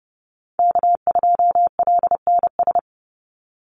Morse code, heard as a keyed tone, about 30 wpm, 700 Hz. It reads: K2LNS
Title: Call Signs - 30wpm